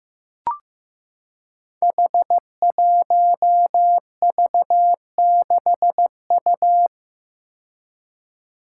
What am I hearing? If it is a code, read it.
Morse code, heard as a keyed tone, about 15 wpm, 700 Hz. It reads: H1V6U
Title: Five Challenging Letter-Number Combinations 15wpm